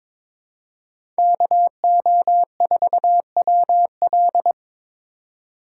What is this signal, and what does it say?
Morse code, heard as a keyed tone, about 22 wpm, 700 Hz. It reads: KO4WL
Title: USA Call Signs - 5 Characters - 3x Character Spacing - 22wpm